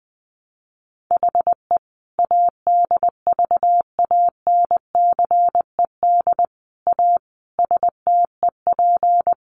Morse code, heard as a keyed tone, about 20 wpm, 700 Hz. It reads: HE ADVANCED A STEP
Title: Sentences from Top 1200 Words 20wpm